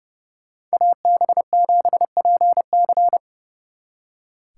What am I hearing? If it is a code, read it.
Morse code, heard as a keyed tone, about 36 wpm, 700 Hz. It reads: AB7PC
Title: Speed Racing - USA 2x2 Call Signs 30wpm